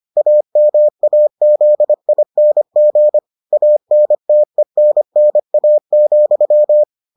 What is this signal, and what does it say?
Morse code, heard as a keyed tone, about 25 wpm, 600 Hz. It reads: AMAZING ANTENNA,